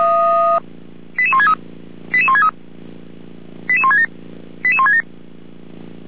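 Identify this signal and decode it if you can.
Five-tone selective call (ZVEI), heard as quick runs of five tones, tones of about 70 ms. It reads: 88163 88163 88167 88167
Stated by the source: ZVEI five-tone selective call 88459 received on SDR.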